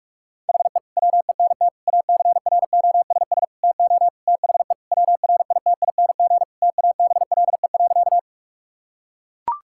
Morse code, heard as a keyed tone, about 41 wpm, 690 Hz. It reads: HE WENT ACROSS TO THE WRITING TABLE.
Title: Sentences from Top 1000 Words - Review Part 5 - 45wpm